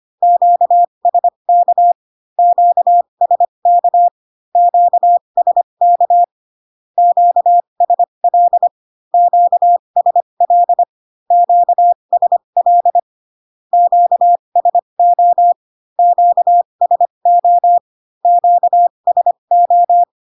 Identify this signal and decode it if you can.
Morse code, heard as a keyed tone, about 25 wpm, spaced as for 22 wpm, 700 Hz. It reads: QSK QSK QSK QSL QSL QSL QSO QSO QSO
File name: Q-codes most common repeated 3 times 18 25wpm